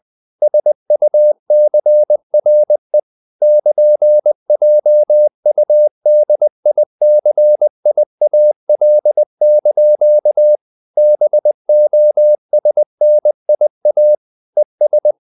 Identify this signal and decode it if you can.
Morse code, heard as a keyed tone, about 20 wpm, 600 Hz. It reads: SUCRE <KN>JUDICIAL) BOSNIA ES
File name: CW / Practice / NationalCapitals20